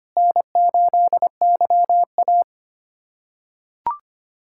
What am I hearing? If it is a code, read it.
Morse code, heard as a keyed tone, about 25 wpm, 700 Hz. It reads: N8YA
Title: USA Callsign DE Callsign - 4 Characters - 25wpm